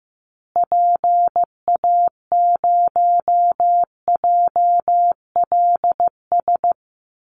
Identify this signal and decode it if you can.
Morse code, heard as a keyed tone, about 14 wpm, 700 Hz. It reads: PA0JLS
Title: Call Signs Repeated 3x - 15wpm